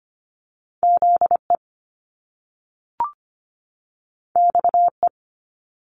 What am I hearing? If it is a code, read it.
Morse code, heard as a keyed tone, about 25 wpm, 700 Hz. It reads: ZE XE
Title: Top Two Letter Combinations in Words 25wpm